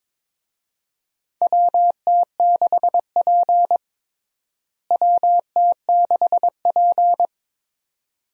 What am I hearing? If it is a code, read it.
Morse code, heard as a keyed tone, about 22 wpm, 700 Hz. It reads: WT6P WT6P
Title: USA Call Signs - 4 Characters - Repeated 2x - 22wpm